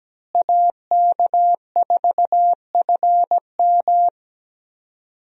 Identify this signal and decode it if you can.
Morse code, heard as a keyed tone, about 17 wpm, 700 Hz. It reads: AK4FM
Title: USA Call Signs - 5 Characters - Repeated 2x - 17wpm